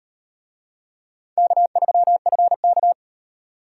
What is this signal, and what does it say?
Morse code, heard as a keyed tone, about 38 wpm, 700 Hz. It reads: K3FK